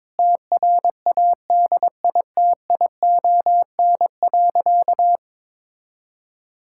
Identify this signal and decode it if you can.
Morse code, heard as a keyed tone, about 22 wpm, 700 Hz. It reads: TRADITION.
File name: Sentences with 7-Word Phrases - 22wpm